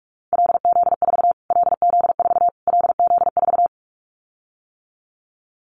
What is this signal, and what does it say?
Morse code, heard as a keyed tone, about 44 wpm, 700 Hz. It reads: LZ4 LZ4 LZ4
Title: Callsign Prefixes for Top 25 DXCC - Repeated 3x - 45wpm